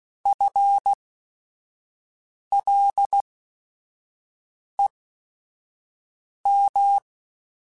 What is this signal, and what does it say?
Morse code, heard as a keyed tone, about 16 wpm, spaced as for 5 wpm, 800 Hz. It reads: FLEM
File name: General / General-mp3s / ISH @5